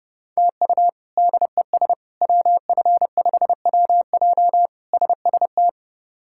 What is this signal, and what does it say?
Morse code, heard as a keyed tone, about 30 wpm, 700 Hz. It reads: TU DES WF5WJ SST
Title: K1USN SST QSO Practice - 30wpm